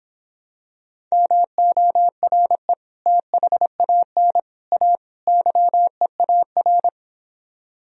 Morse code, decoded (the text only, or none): MORE THAN A YEAR